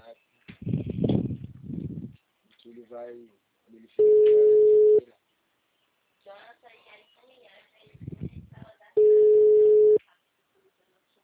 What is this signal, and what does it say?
Ringback tone (Europe, 425 Hz), with speech.